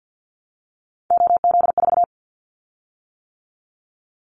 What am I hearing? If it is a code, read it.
Morse code, heard as a keyed tone, about 50 wpm, 700 Hz. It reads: OZ4
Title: Callsign Prefixes for Top 50 DXCC - 50wpm